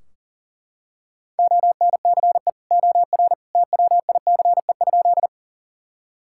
Morse code, heard as a keyed tone, about 40 wpm, 700 Hz. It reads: ONCE OR TWICE?
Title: Sentences from Top 1600 Words 40wpm